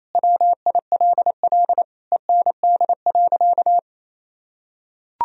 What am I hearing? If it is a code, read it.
Morse code, heard as a keyed tone, about 28 wpm, 700 Hz. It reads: WILL END.
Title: Sentences from Top 1000 Words - Review Part 1 - 28wpm